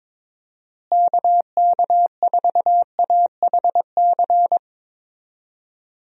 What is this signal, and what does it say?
Morse code, heard as a keyed tone, about 22 wpm, 700 Hz. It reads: KK4AHC
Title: USA Call Signs - 6 Characters - Repeated 3x - 22wpm